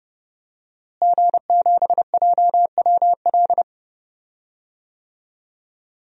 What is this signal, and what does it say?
Morse code, heard as a keyed tone, about 29 wpm, 700 Hz. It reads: G7JWL
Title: DXCC-100 Call Signs - 5 Characters - Repeated 3x - 30wpm